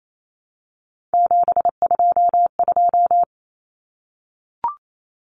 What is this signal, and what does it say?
Morse code, heard as a keyed tone, about 28 wpm, 700 Hz. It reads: 722
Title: Numbers - 3-Digit 28wpm